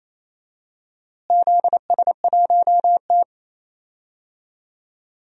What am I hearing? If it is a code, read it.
Morse code, heard as a keyed tone, about 28 wpm, 700 Hz. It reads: ZS1T
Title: DXCC-100 Call Signs - 4 Characters - Repeated 3x - 28wpm